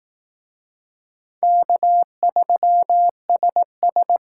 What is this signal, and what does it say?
Morse code, heard as a keyed tone, about 18 wpm, 700 Hz. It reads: K3SS
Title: Speed Racing - USA 1x2 Call Signs 15wpm